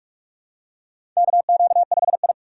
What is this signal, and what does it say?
Morse code, heard as a keyed tone, about 44 wpm, 700 Hz. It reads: KQ5I
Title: USA Call Signs - 4 Characters - 6x Character Spacing - 45wpm